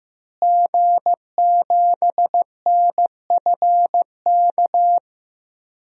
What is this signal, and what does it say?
Morse code, heard as a keyed tone, about 15 wpm, 700 Hz. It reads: G7NFK
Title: DXCC-100 Call Signs - 5 Characters - Repeated 3x - 15wpm